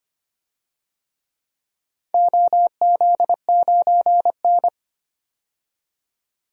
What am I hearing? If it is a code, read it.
Morse code, heard as a keyed tone, about 25 wpm, 700 Hz. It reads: OZ9N
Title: DXCC-100 Call Signs - 4 Characters - 25wpm